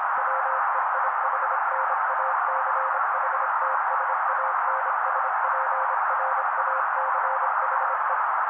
Morse code, heard as a keyed tone, about 26 wpm, 530 Hz. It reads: WEIHNACHTSANSPRACHE
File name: daily / 25wpm noise